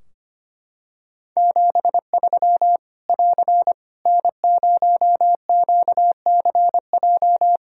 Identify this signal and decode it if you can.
Morse code, heard as a keyed tone, about 25 wpm, 700 Hz. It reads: 73 <AR> N0QCJ
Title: K1USN SST QSO Practice 25wpm